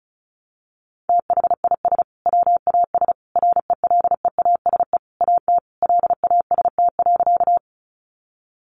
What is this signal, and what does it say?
Morse code, heard as a keyed tone, about 34 wpm, 700 Hz. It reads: THIS WAS RELEASE AT LAST.
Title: Sentences from Top 1100 Words 35wpm